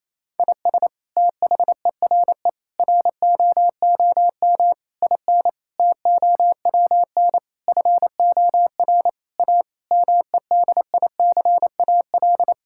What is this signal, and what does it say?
Morse code, heard as a keyed tone, about 28 wpm, 700 Hz. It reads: IS THERE ROOM IN TOWN FOR A MEDICAL
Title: Sentences from Top 1000 Words - Review Part 5 - 28wpm